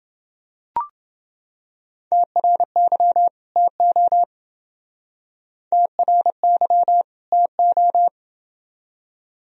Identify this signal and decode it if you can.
Morse code, heard as a keyed tone, about 29 wpm, 700 Hz. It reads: TRY TO TRY TO